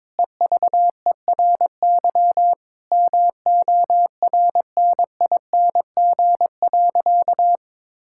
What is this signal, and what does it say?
Morse code, heard as a keyed tone, about 22 wpm, 700 Hz. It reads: EVERY MORNING.
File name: practice / 22wpm / Sentences with 2-Word Phrases - 22wpm